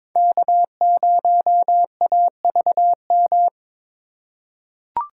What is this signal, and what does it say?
Morse code, heard as a keyed tone, about 22 wpm, 700 Hz. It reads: K0AVM
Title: USA Call Signs - 5 Characters - 4x Character Spacing - 22wpm